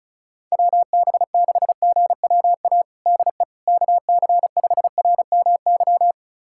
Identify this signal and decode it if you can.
Morse code, heard as a keyed tone, about 36 wpm, 700 Hz. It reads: WB6GWA DE KC5RMY
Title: USA Callsign DE Callsign 35wpm